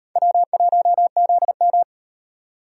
Morse code, heard as a keyed tone, about 37 wpm, 700 Hz. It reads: W1ZM